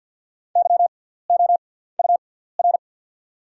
Morse code, heard as a keyed tone, about 48 wpm, spaced as for 17 wpm, 690 Hz. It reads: YOUR
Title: 6x Character Spacing - Top 200 Words - 50wpm